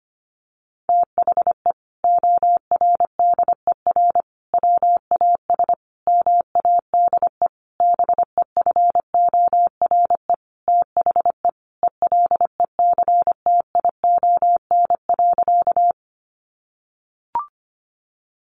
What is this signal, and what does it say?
Morse code, heard as a keyed tone, about 25 wpm, 700 Hz. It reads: THE ORDER WAS MADE BEFORE THE ELECTION.
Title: Sentences from Top 1500 Words 25wpm